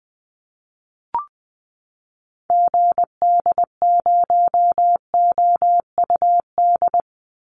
Morse code, heard as a keyed tone, about 20 wpm, 700 Hz. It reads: GD0OUD
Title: Call Signs Repeated 2x - 20wpm